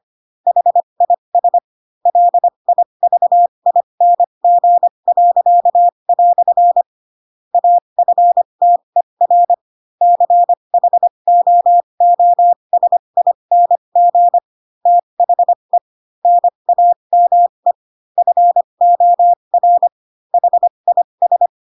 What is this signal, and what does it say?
Morse code, heard as a keyed tone, about 25 wpm, spaced as for 22 wpm, 700 Hz. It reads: HIS LIVING." AFTER CHOOSING THE NAME FOR HIS
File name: Pinocchio / Pinocchio ch3 18 25 ik1hge